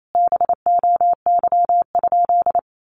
Morse code, heard as a keyed tone, about 28 wpm, 700 Hz. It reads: BOY?
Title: Sentences from Top 900 Words 28wpm